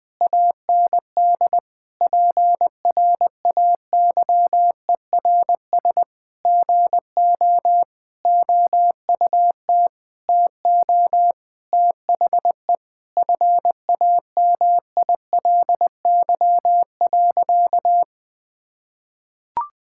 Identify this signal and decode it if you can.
Morse code, heard as a keyed tone, about 20 wpm, 700 Hz. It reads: AND PRAYERS GO OUT TO THE FAMILY.
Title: Sentences with 8-Word Phrases - Spoken Beginning - 20wpm